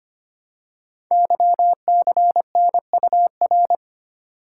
Morse code, heard as a keyed tone, about 25 wpm, 700 Hz. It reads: YCNUR
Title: Code Groups - 5 Letter:Numbers - 25wpm